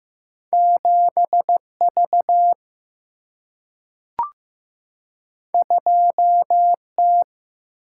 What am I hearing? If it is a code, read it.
Morse code, heard as a keyed tone, about 15 wpm, 700 Hz. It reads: 7V 2T
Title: Single-Letter Single-Number 15wpm